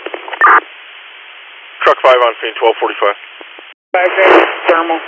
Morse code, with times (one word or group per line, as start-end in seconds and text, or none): none